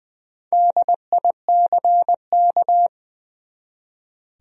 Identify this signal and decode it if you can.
Morse code, heard as a keyed tone, about 20 wpm, 700 Hz. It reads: DICK
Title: USA Common Names - 20wpm